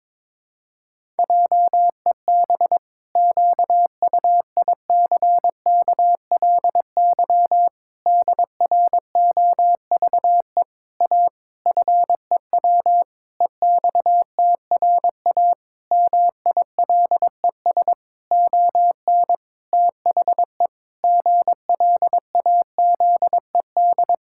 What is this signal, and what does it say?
Morse code, heard as a keyed tone, about 22 wpm, 700 Hz. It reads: JEB QUICKLY DROVE A FEW EXTRA MILES ON THE GLAZED
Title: Pangram Challenge v2 - 22wpm